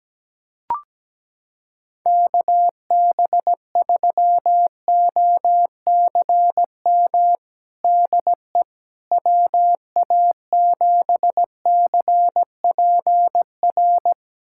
Morse code, heard as a keyed tone, about 17 wpm, 700 Hz. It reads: KB3OCM DE WA7CPR
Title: USA Callsign DE Callsign 17wpm